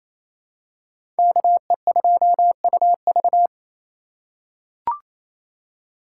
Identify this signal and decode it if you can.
Morse code, heard as a keyed tone, about 28 wpm, 690 Hz. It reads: KE2UV
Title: USA Call Signs - 5 Characters - 4x Character Spacing - 28wpm